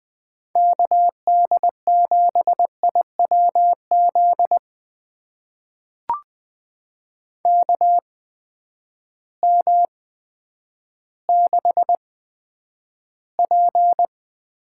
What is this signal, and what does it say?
Morse code, heard as a keyed tone, about 20 wpm, spaced as for 10 wpm, 700 Hz. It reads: KD7IWZ KM6P
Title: USA Call Signs - 6 Characters - 8x Character Spacing - 20wpm